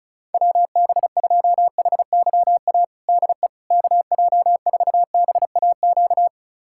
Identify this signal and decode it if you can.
Morse code, heard as a keyed tone, about 36 wpm, 700 Hz. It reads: WB2HYA DE KJ4BAQ
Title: USA Callsign DE Callsign - 6 Characters - 35wpm